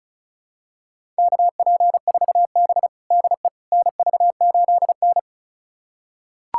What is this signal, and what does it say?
Morse code, heard as a keyed tone, about 36 wpm, 700 Hz. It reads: KP4B DE NV8N
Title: USA Callsign DE Callsign - 4 Characters - 35wpm